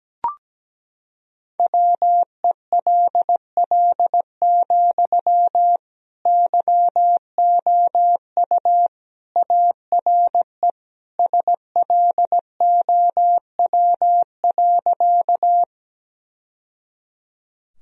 Morse code, heard as a keyed tone, about 17 wpm, 700 Hz. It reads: WELL, YOU ARE SLOW.
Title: Sentences from Top 1000 Words - Review Part 2 - 17wpm